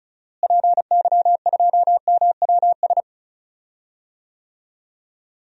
Morse code, heard as a keyed tone, about 34 wpm, 700 Hz. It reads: PY2MWS
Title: DXCC-100 Call Signs - 6 Characters - Repeated 2x - 35wpm